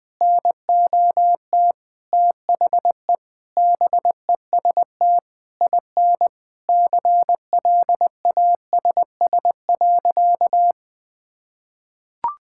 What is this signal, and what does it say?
Morse code, heard as a keyed tone, about 20 wpm, 700 Hz. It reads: NOT THE BEST IN CLASS.
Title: Sentences with 8-Word Phrases - Spoken Beginning - 20wpm